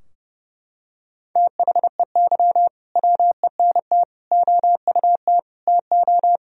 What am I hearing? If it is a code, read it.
Morse code, heard as a keyed tone, about 30 wpm, 700 Hz. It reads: THEY WENT OUT TO
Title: Sentences from Top 1000 Words - Review Part 2 - 30wpm